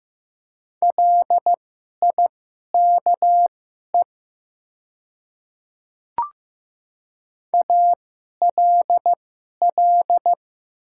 Morse code, heard as a keyed tone, about 15 wpm, spaced as for 11 wpm, 700 Hz. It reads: LIKE ALL
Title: 2x Character Spacing - Top 100 Words - 15wpm